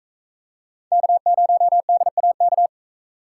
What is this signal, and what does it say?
Morse code, heard as a keyed tone, about 41 wpm, 700 Hz. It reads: K0DAK